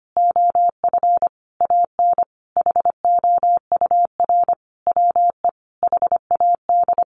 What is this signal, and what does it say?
Morse code, heard as a keyed tone, about 25 wpm, 700 Hz. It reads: OF AN HOUR WE HAD
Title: Sentences from Top 1300 Words 25wpm